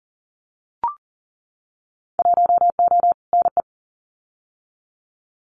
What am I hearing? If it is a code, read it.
Morse code, heard as a keyed tone, about 40 wpm, 700 Hz. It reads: 1O NE
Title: Winter Field Day Exchanges Encore 40wpm